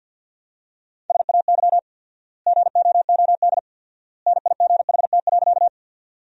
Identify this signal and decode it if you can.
Morse code, heard as a keyed tone, about 50 wpm, 700 Hz. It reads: SAY GOOD NIGHT.
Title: 4x Word-Spacing - Sentences from Top 300 Words - 50wpm